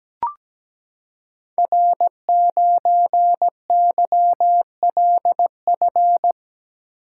Morse code, heard as a keyed tone, about 17 wpm, 700 Hz. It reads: R9YLF
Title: Code Groups - 5 Letter:Numbers - 17wpm